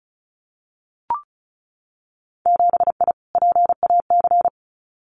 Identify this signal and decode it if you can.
Morse code, heard as a keyed tone, about 36 wpm, 700 Hz. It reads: 7I PAC
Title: Winter Field Day Exchanges Encore 35wpm